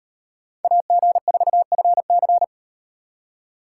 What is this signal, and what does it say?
Morse code, heard as a keyed tone, about 38 wpm, 700 Hz. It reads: AG4FC